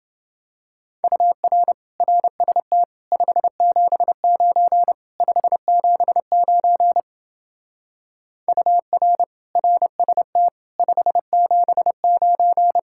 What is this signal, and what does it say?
Morse code, heard as a keyed tone, about 28 wpm, 700 Hz. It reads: UR RST 579 579 UR RST 579